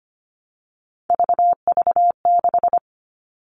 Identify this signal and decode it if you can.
Morse code, heard as a keyed tone, about 25 wpm, 700 Hz. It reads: VV6
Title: All Cut Numbers - 3-Digit 25wpm